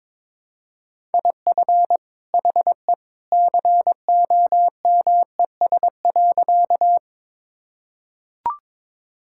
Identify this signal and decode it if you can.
Morse code, heard as a keyed tone, about 22 wpm, 700 Hz. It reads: IF HE COMES.
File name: practice / 22wpm / Sentences with 2-Word Phrases - 22wpm